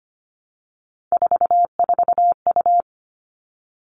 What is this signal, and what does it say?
Morse code, heard as a keyed tone, about 25 wpm, 700 Hz. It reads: 44U
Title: All Cut Numbers - 3-Digit 25wpm